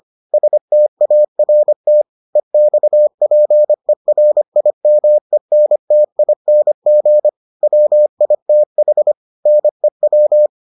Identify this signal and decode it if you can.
Morse code, heard as a keyed tone, about 25 wpm, 600 Hz. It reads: START EXPERIMENTING WITH NEW